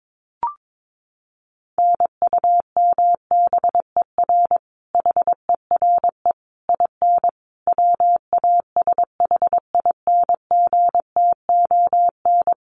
Morse code, heard as a keyed tone, about 22 wpm, 700 Hz. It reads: NUMBER HERE IN WASHINGTON
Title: Sets of 5 Words Encore2 22wpm